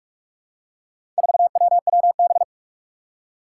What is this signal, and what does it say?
Morse code, heard as a keyed tone, about 46 wpm, 700 Hz. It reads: 4WWB
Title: Four Challenging Letter-Number Combinations 45wpm